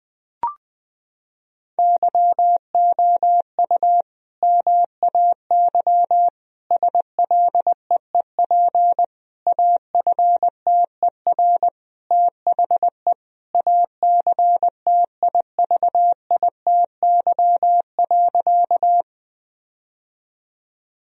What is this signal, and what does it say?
Morse code, heard as a keyed tone, about 20 wpm, 700 Hz. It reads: YOU MAY SLEEP AFTER THE ACTIVITY.
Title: Sentences from Top 1000 Words - Review Part 4 - 20wpm